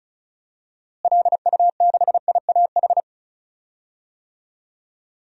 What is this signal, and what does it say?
Morse code, heard as a keyed tone, about 36 wpm, 700 Hz. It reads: LU6IAH
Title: DXCC-100 Call Signs - 6 Characters - Repeated 2x - 35wpm